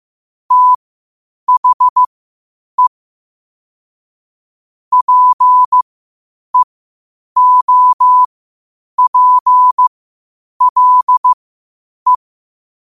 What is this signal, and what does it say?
Morse code, heard as a keyed tone, about 15 wpm, spaced as for 8 wpm, 1000 Hz. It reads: THE PEOPLE
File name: qotdfiles / 8wpm